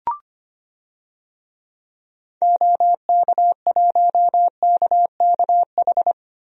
Morse code, heard as a keyed tone, about 25 wpm, 690 Hz. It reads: OK1KKH
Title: DXCC-100 Call Signs - 6 Characters - 25wpm